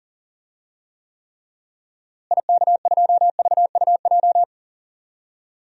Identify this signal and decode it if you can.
Morse code, heard as a keyed tone, about 40 wpm, 700 Hz. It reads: IK2VUJ
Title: DXCC-100 Call Signs - 6 Characters - Repeated 3x - 40wpm